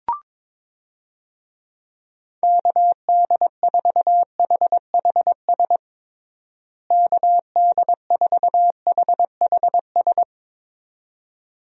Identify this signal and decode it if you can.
Morse code, heard as a keyed tone, about 22 wpm, 700 Hz. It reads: KD4HHS KD4HHS
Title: USA Call Signs - 6 Characters - Repeated 2x - 22wpm